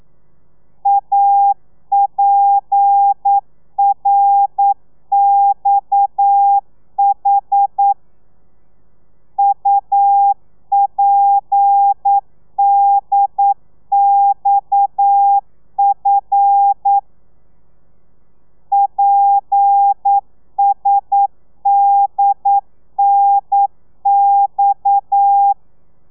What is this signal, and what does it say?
Morse code, heard as a keyed tone, about 9 wpm, 800 Hz. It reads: APRXH UPDXF PSDNX
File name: telegrafia / i2tfj / Lez7 40